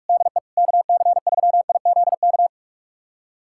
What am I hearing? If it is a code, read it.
Morse code, heard as a keyed tone, about 42 wpm, 700 Hz. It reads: DE KC3IZK
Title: USA Callsign DE Callsign 45wpm